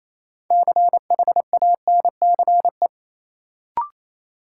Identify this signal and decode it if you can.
Morse code, heard as a keyed tone, about 28 wpm, 700 Hz. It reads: CHANCE
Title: Sets of 3 Words - New Word Drill - 28wpm